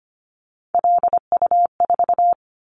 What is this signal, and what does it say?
Morse code, heard as a keyed tone, about 25 wpm, 700 Hz. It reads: LU4
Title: Callsign Prefixes for Top 100 DXCC - 25wpm